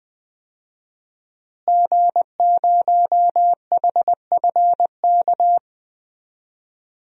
Morse code, heard as a keyed tone, about 20 wpm, 700 Hz. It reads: G0HFK
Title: DXCC-100 Call Signs - 5 Characters - Repeated 3x - 20wpm